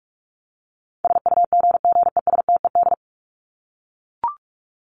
Morse code, heard as a keyed tone, about 44 wpm, 700 Hz. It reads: SUGGESTED
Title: ED Suffix - Top 100 Words with Base Words 45wpm